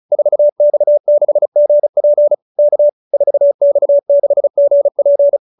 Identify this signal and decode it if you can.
Morse code, heard as a keyed tone, about 34 wpm, 590 Hz. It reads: 4X6GP K 4X6GP